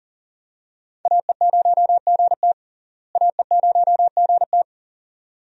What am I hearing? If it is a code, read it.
Morse code, heard as a keyed tone, about 40 wpm, 700 Hz. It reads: AE0GT AE0GT
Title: USA Call Signs - 5 Characters - Repeated 2x - 40wpm